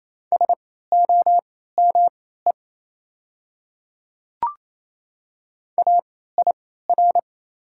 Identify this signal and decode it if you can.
Morse code, heard as a keyed tone, about 28 wpm, spaced as for 16 wpm, 700 Hz. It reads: SOME AIR
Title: 3x Character Spacing - Top 200 Words - 28wpm